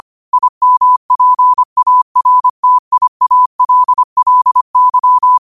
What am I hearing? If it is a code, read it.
Morse code, heard as a keyed tone, about 25 wpm, 1000 Hz. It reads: IMPARTIALLY
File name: qotdfiles / 25wpm